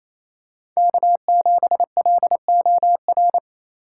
Morse code, heard as a keyed tone, about 28 wpm, 700 Hz. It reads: K7LOR
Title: USA Call Signs - 5 Characters - 6x Character Spacing - 28wpm